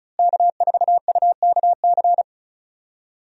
Morse code, heard as a keyed tone, about 36 wpm, 700 Hz. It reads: K4UKC
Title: USA Call Signs - 5 Characters - Repeated 3x - 35wpm